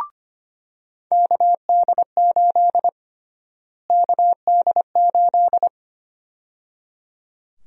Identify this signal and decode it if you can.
Morse code, heard as a keyed tone, about 25 wpm, 700 Hz. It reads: KD8 KD8
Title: USA Call Sign Prefixes - Repeated 2x - 25wpm